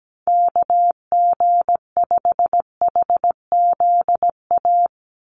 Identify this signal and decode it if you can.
Morse code, heard as a keyed tone, about 17 wpm, 700 Hz. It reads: KG5HZA
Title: USA Call Signs 17wpm